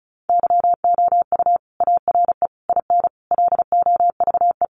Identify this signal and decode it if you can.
Morse code, heard as a keyed tone, about 34 wpm, 700 Hz. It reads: YOU ARE IN LOVE
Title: Sentences from Top 400 Words 35wpm